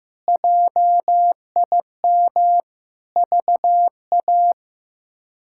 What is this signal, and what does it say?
Morse code, heard as a keyed tone, about 15 wpm, 700 Hz. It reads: JIM VA
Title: K1USN SST QSO Practice - 15wpm